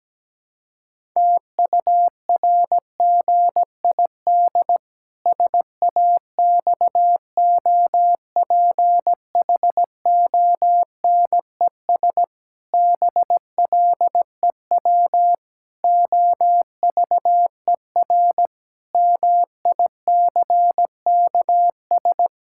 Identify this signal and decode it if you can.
Morse code, heard as a keyed tone, about 17 wpm, 700 Hz. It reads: TURGID SAXOPHONES BLEW OVER MICKS
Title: Pangram Challenge v2 - 17wpm